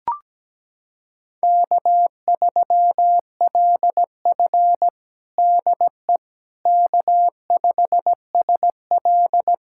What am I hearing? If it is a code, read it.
Morse code, heard as a keyed tone, about 17 wpm, 700 Hz. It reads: K3LF DE K5SL
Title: USA Callsign DE Callsign - 4 Characters - 17wpm